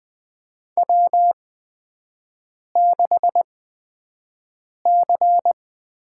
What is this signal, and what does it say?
Morse code, heard as a keyed tone, about 20 wpm, spaced as for 5 wpm, 700 Hz. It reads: W6C
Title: USA Call Signs - 4 Characters - 8x Character Spacing - 20wpm